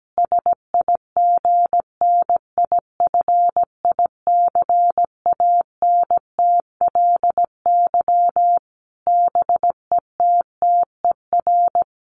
Morse code, heard as a keyed tone, about 17 wpm, 700 Hz. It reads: SIGNIFICANTLY BETTER
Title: Sentences from Top 1500 Words 17wpm